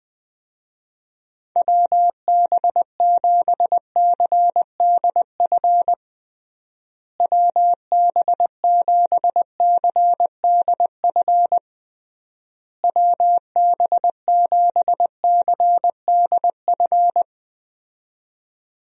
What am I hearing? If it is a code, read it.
Morse code, heard as a keyed tone, about 20 wpm, 700 Hz. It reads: WB7CDF WB7CDF WB7CDF
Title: USA Call Signs - 6 Characters - Repeated 3x - 20wpm